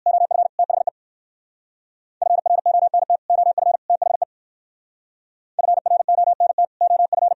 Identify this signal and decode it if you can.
Morse code, heard as a keyed tone, about 57 wpm, 700 Hz. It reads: OF THE FRONT OF THE FRONT OF